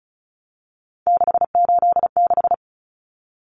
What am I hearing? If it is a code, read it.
Morse code, heard as a keyed tone, about 34 wpm, 700 Hz. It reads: 686
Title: Common Cut Numbers - 3-Digit 35wpm